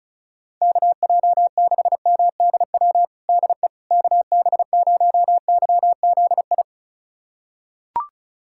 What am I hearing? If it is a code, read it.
Morse code, heard as a keyed tone, about 36 wpm, 700 Hz. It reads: KJ6MDW DE KB0YZI
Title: USA Callsign DE Callsign 35wpm